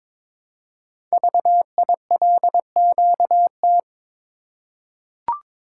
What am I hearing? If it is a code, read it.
Morse code, heard as a keyed tone, about 22 wpm, 700 Hz. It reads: VILQT
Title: Code Groups - 5 Letters - 22wpm